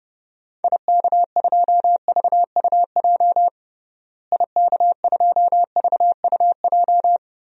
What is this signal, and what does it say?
Morse code, heard as a keyed tone, about 30 wpm, 700 Hz. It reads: IK2VUJ IK2VUJ
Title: DXCC-100 Call Signs - 6 Characters - Repeated 3x - 30wpm